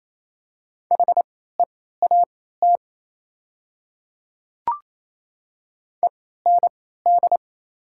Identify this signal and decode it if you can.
Morse code, heard as a keyed tone, about 27 wpm, spaced as for 16 wpm, 690 Hz. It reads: HEAT END
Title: 3x Character Spacing - Top 500 Words - 28wpm